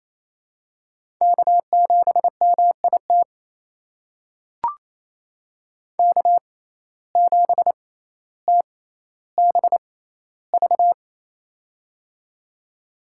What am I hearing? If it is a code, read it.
Morse code, heard as a keyed tone, about 28 wpm, spaced as for 14 wpm, 700 Hz. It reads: K7MIT K7TBV
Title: USA Call Signs - 5 Characters - 6x Character Spacing - 28wpm